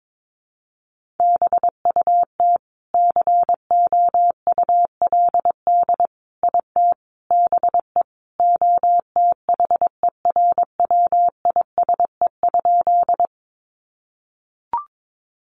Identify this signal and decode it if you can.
Morse code, heard as a keyed tone, about 22 wpm, 700 Hz. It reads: BUT COULD IT BE OTHERWISE?
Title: Sentences from Top 1200 Words 22wpm